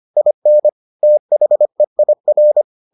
Morse code, heard as a keyed tone, about 25 wpm, 600 Hz. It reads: IN THEIR